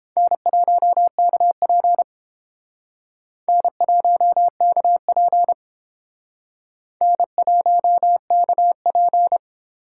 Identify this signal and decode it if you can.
Morse code, heard as a keyed tone, about 30 wpm, 700 Hz. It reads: N1KP N1KP N1KP